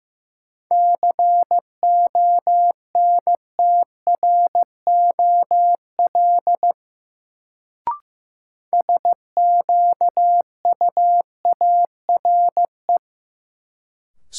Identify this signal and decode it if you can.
Morse code, heard as a keyed tone, about 15 wpm, 700 Hz. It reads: CONTROL SQUARE
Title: Sentences from Top 1200 Words - New Word Drill Alt - 15wpm